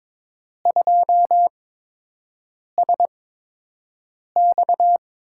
Morse code, heard as a keyed tone, about 22 wpm, spaced as for 6 wpm, 700 Hz. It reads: 2SX
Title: USA Call Signs - 4 Characters - 8x Character Spacing - 22wpm